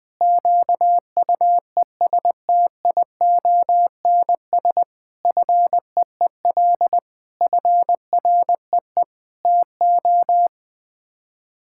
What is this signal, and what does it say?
Morse code, heard as a keyed tone, about 20 wpm, 700 Hz. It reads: QUESTIONS FEEL FREE TO
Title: Sets of 8 Words - 20wpm